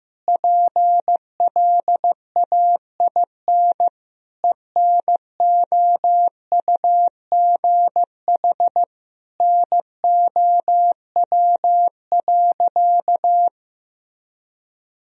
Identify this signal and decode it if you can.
Morse code, heard as a keyed tone, about 15 wpm, 700 Hz. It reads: PLAIN ENOUGH NOW.
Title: Sentences from Top 1000 Words - Review Part 2 - 15wpm